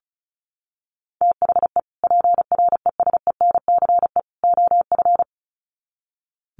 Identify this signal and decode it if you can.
Morse code, heard as a keyed tone, about 34 wpm, 700 Hz. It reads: THE PRESENCE OF
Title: Sets of 3 Words 35wpm